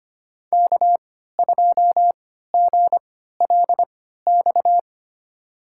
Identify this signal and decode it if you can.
Morse code, heard as a keyed tone, about 25 wpm, spaced as for 14 wpm, 700 Hz. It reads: K2GLX
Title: USA Call Signs - 3x Character Spacing - 25wpm